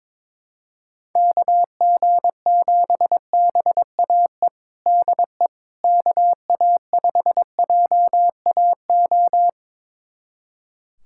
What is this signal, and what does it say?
Morse code, heard as a keyed tone, about 22 wpm, 700 Hz. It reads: KG7BAE DE KA5JAO
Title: USA Callsign DE Callsign - 6 Characters - 22wpm